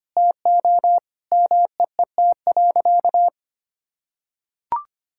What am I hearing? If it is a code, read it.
Morse code, heard as a keyed tone, about 25 wpm, 700 Hz. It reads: TO MEET.
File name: practice / 25wpm / Sentences with 3-Word Phrases - 25wpm